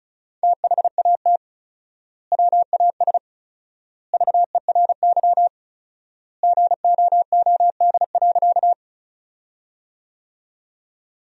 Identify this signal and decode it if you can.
Morse code, heard as a keyed tone, about 34 wpm, 700 Hz. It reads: THAT WAS VERY GOOD.
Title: 4x Word-Spacing - Sentences from Top 200 Words - 35wpm